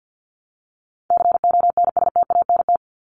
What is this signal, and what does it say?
Morse code, heard as a keyed tone, about 50 wpm, 700 Hz. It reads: CONSTANT
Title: LY Suffix - Top 100 Words with Base Words 50wpm